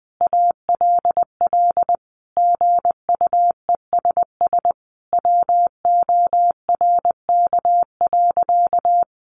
Morse code, heard as a keyed tone, about 20 wpm, 700 Hz. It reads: ALL GUESS WORK.
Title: Sentences from Top 1500 Words 20wpm